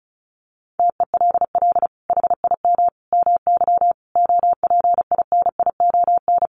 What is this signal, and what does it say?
Morse code, heard as a keyed tone, about 34 wpm, 700 Hz. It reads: TELL HIM MY OPINION
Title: Sentences from Top 1200 Words 35wpm